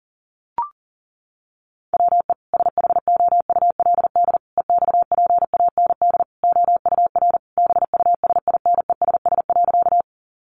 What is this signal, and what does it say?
Morse code, heard as a keyed tone, about 40 wpm, 700 Hz. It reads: WE SHOULD EXPAND OUR BUSINESS.
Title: Sentences from Top 1400 Words 40wpm